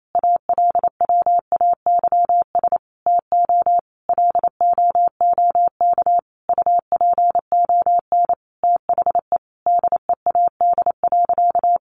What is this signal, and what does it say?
Morse code, heard as a keyed tone, about 28 wpm, 700 Hz. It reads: ALWAYS TO LOOK UPON THE DEAD.
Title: Sentences from Top 1500 Words 28wpm